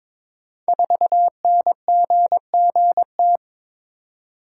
Morse code, heard as a keyed tone, about 22 wpm, 700 Hz. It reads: 4NGGT
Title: Code Groups - 5 Letter:Numbers - 22wpm